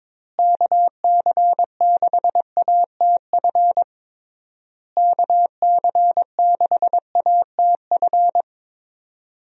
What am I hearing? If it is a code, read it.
Morse code, heard as a keyed tone, about 22 wpm, 700 Hz. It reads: KC6ATF KC6ATF
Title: USA Call Signs - 6 Characters - Repeated 3x - 22wpm